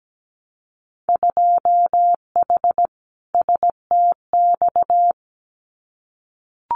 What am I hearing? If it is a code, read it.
Morse code, heard as a keyed tone, about 17 wpm, 700 Hz. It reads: 2H STX
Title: Winter Field Day Exchanges Encore 17wpm